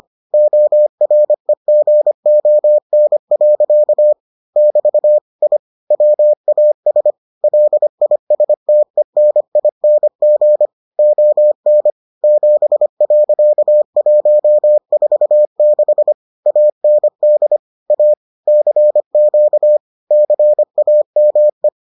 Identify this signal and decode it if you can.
Morse code, heard as a keyed tone, about 25 wpm, 600 Hz. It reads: OREGON. <BT> I WAS LISTENING ON 7.146 AND A CQ CAME